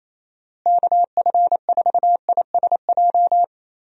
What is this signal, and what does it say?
Morse code, heard as a keyed tone, about 28 wpm, 700 Hz. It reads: KF4ISJ
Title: USA Call Signs - 6 Characters - Repeated 3x - 28wpm